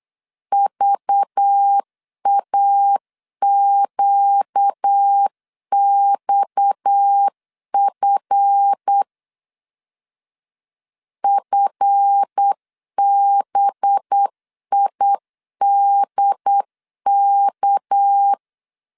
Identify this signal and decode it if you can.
Morse code, heard as a keyed tone, about 8 wpm, 800 Hz. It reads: VAQXF FBIDK
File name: LEZIONI / 7 GF9Z / 7 GF 8wpm 3108